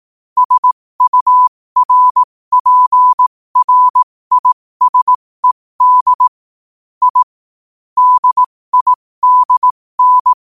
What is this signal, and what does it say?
Morse code, heard as a keyed tone, about 18 wpm, spaced as for 15 wpm, 1000 Hz. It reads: SURPRISED I DIDN
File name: qotdfiles / 15wpm